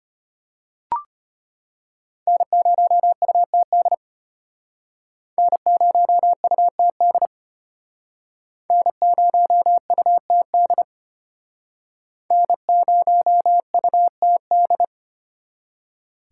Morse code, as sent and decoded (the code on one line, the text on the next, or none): -. ----- ..- - -.. / -. ----- ..- - -.. / -. ----- ..- - -.. / -. ----- ..- - -..
N0UTD N0UTD N0UTD N0UTD